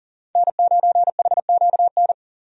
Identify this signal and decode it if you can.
Morse code, heard as a keyed tone, about 40 wpm, 700 Hz. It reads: N9HQN
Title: USA Call Signs - 5 Characters - 40wpm